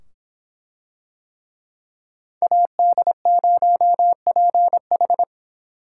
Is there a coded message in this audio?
Yes: Morse code from 2.4 to 5.2 s, heard as a keyed tone, about 25 wpm, 700 Hz. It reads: AD0PH